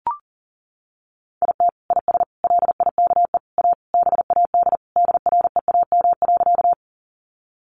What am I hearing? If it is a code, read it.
Morse code, heard as a keyed tone, about 40 wpm, 700 Hz. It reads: IT IS LIKE A BAD DREAM.
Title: Sentences from Top 1500 Words 40wpm